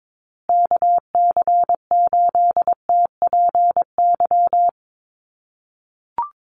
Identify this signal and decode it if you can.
Morse code, heard as a keyed tone, about 22 wpm, 700 Hz. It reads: KC8TPY
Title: USA Call Signs Repeated 3x 22wpm